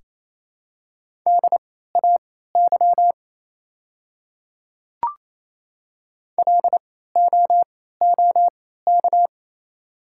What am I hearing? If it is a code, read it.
Morse code, heard as a keyed tone, about 28 wpm, spaced as for 16 wpm, 700 Hz. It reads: DAY LOOK
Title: 3x Character Spacing - Top 100 Words - 28wpm